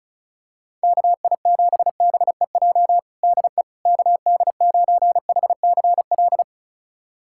Morse code, heard as a keyed tone, about 36 wpm, 700 Hz. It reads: KI7BEJ DE KD9HCL
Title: USA Callsign DE Callsign - 6 Characters - 35wpm